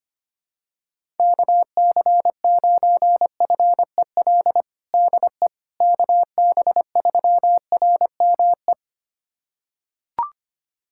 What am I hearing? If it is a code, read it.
Morse code, heard as a keyed tone, about 25 wpm, 700 Hz. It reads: KC9FEL DE KB3RME
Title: USA Callsign DE Callsign - 6 Characters - 25wpm